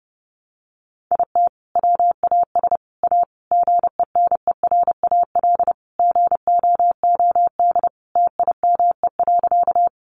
Morse code, heard as a keyed tone, about 30 wpm, 700 Hz. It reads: IT WAS A GENERAL GOOD TIME.
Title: Sentences from Top 600 Words 30wpm